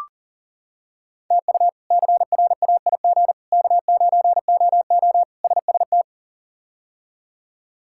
Morse code, heard as a keyed tone, about 37 wpm, 700 Hz. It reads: TU CRAIG K9OO SST
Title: K1USN SST QSO Practice - 40wpm